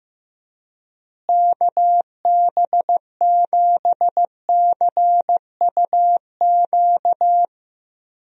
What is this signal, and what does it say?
Morse code, heard as a keyed tone, about 15 wpm, 700 Hz. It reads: KB7CUQ
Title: USA Call Signs - 6 Characters - Repeated 3x - 15wpm